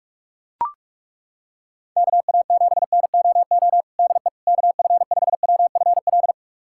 Morse code, heard as a keyed tone, about 41 wpm, 700 Hz. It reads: KA7NOO DE KF5WFL
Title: USA Callsign DE Callsign 45wpm